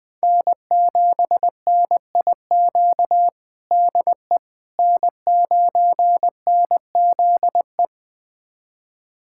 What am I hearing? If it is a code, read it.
Morse code, heard as a keyed tone, about 20 wpm, 700 Hz. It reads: N7NIQ DE N9NZE
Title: USA Callsign DE Callsign - 5 Characters - 20wpm